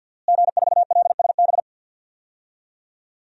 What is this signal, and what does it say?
Morse code, heard as a keyed tone, about 47 wpm, 700 Hz. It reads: G4LSB
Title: DXCC-100 Call Signs - 5 Characters - Repeated 3x - 50wpm